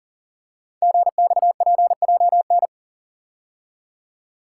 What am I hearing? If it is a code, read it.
Morse code, heard as a keyed tone, about 41 wpm, 690 Hz. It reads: GXPJN
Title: Code Groups - 5 Letter:Numbers - 40wpm